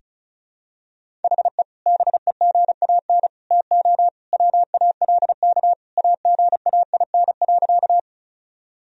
Morse code, heard as a keyed tone, about 36 wpm, 700 Hz. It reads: HE BEGAN TO WALK AGAIN.
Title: Sentences from Top 1000 Words - Review Part 1 - 35wpm